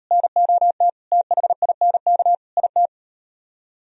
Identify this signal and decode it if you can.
Morse code, heard as a keyed tone, about 38 wpm, 700 Hz. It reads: NOT THINK IT